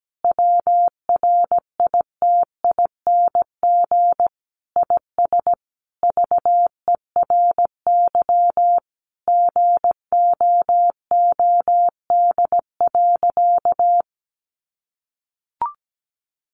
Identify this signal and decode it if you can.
Morse code, heard as a keyed tone, about 17 wpm, 700 Hz. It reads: WRITING IS VERY GOOD.
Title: Sentences from Top 800 Words 17wpm